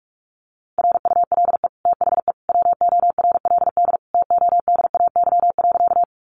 Morse code, heard as a keyed tone, about 44 wpm, 700 Hz. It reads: RULE THE WORLD TODAY.
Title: Sentences from Top 600 Words 45wpm